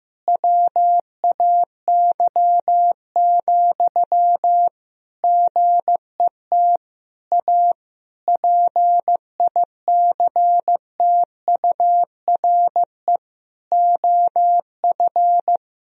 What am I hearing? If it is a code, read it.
Morse code, heard as a keyed tone, about 15 wpm, 700 Hz. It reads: WAY, GET A PICTURE OF
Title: Sentences from Top 1000 Words - Review Part 1 - 15wpm